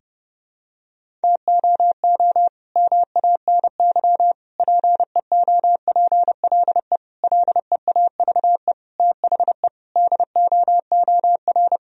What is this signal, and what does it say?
Morse code, heard as a keyed tone, about 30 wpm, 700 Hz. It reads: TOO MANY PEOPLE LEAVE THE DOOR
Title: Sentences from Top 1000 Words - Review Part 1 - 30wpm